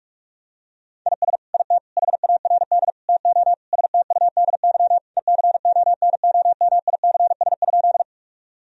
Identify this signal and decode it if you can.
Morse code, heard as a keyed tone, about 43 wpm, 700 Hz. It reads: IS IT HARD TO STUDY ECONOMICS?
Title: Sentences from Top 1000 Words - Review Part 3 - 45wpm